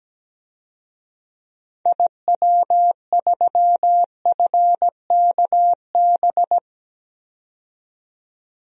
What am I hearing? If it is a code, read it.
Morse code, heard as a keyed tone, about 17 wpm, 700 Hz. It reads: IW3FKB
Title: DXCC-100 Call Signs - 6 Characters - 17wpm